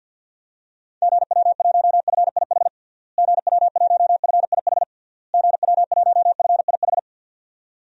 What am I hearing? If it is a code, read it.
Morse code, heard as a keyed tone, about 52 wpm, 700 Hz. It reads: GW1FIH GW1FIH GW1FIH
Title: DXCC-100 Call Signs - 6 Characters - Repeated 3x - 50wpm